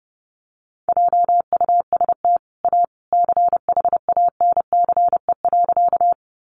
Morse code, heard as a keyed tone, about 30 wpm, 700 Hz. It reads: JUST A CHANCE.
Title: Sentences from Top 800 Words 30wpm